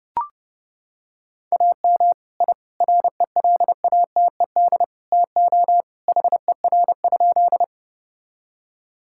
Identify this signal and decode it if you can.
Morse code, heard as a keyed tone, about 30 wpm, 700 Hz. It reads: AM I RELATED TO HER?
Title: Sentences from Top 1000 Words - Review Part 5 - 30wpm